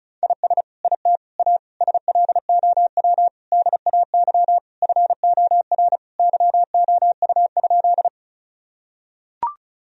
Morse code, heard as a keyed tone, about 36 wpm, 700 Hz. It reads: IS IT A SLOW DAY FOR YOU?
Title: Sentences from Top 1000 Words - Review Part 2 - 35wpm